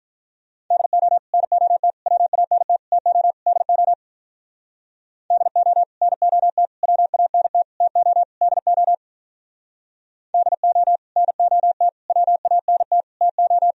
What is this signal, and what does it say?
Morse code, heard as a keyed tone, about 45 wpm, 700 Hz. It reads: DO NOT WANT TO DO DO NOT WANT TO DO DO NOT WANT TO